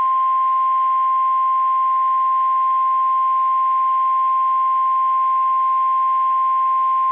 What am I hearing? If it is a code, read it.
Emergency Alert System (SAME) tones (the 1050 Hz attention tone). It is heard from 0.6 to 6.8 s.
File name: eas / KSDZKDJL / KSDZKDJL